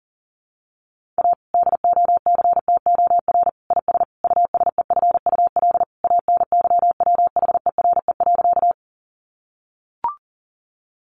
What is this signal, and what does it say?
Morse code, heard as a keyed tone, about 40 wpm, 700 Hz. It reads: A DOCTOR IS USEFUL ANYWHERE.
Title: Sentences from Top 1500 Words 40wpm